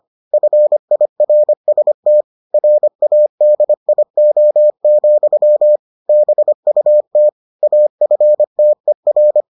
Morse code, heard as a keyed tone, about 25 wpm, 600 Hz. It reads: FIRST RADIO, BUT AFTER